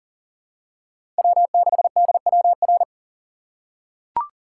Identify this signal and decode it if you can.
Morse code, heard as a keyed tone, about 38 wpm, 700 Hz. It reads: W6DWR
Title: USA Call Signs - 5 Characters - 3x Character Spacing - 40wpm